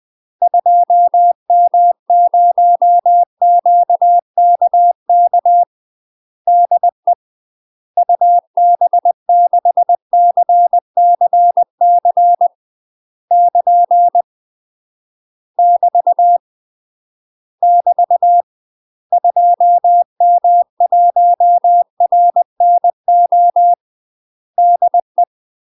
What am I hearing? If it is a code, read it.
Morse code, heard as a keyed tone, about 20 wpm, 700 Hz. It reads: 2M0QKK DE UB6CCC <KN> <BT> <BT> 2M1RNO DE
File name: UK MASTER QSOs TRAINING/20WPM